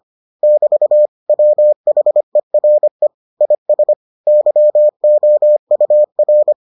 Morse code, heard as a keyed tone, about 25 wpm, 600 Hz. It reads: <BT> WHERE IS YOUR